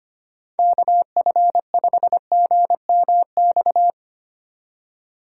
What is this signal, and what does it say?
Morse code, heard as a keyed tone, about 25 wpm, 700 Hz. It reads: KF5GMX
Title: USA Call Signs - 4x Character Spacing - 25wpm